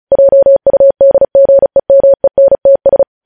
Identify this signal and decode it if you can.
Morse code, heard as a keyed tone, about 36 wpm, 550 Hz. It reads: JUDGEMENTS
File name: podcast / News headlines-1x-35x35